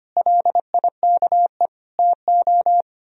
Morse code, heard as a keyed tone, about 31 wpm, 700 Hz. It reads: LIKE TO
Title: Speed-Racing - Sets of 3 Words 25wpm